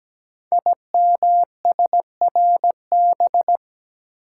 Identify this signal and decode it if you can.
Morse code, heard as a keyed tone, about 17 wpm, 700 Hz. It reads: IMSRB
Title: Code Groups - 5 Letters - 17wpm